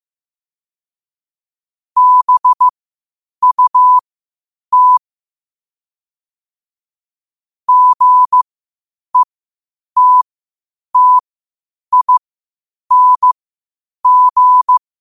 Morse code, heard as a keyed tone, about 15 wpm, spaced as for 8 wpm, 1000 Hz. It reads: BUT GETTING
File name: qotdfiles / 8wpm